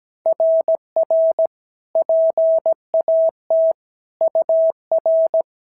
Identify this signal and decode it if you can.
Morse code, heard as a keyed tone, about 16 wpm, 650 Hz. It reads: RR PAT UR
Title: SOTA QSO Practice 17wpm